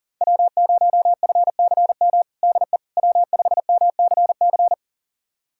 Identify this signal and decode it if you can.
Morse code, heard as a keyed tone, about 40 wpm, 700 Hz. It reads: W0FCM DE W5MCC
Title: USA Callsign DE Callsign - 5 Characters - 40wpm